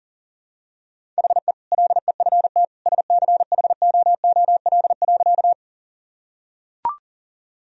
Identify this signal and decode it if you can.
Morse code, heard as a keyed tone, about 37 wpm, 700 Hz. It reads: HE LEFT SCHOOL.
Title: Sentences from Top 1000 Words - Review Part 1 - 40wpm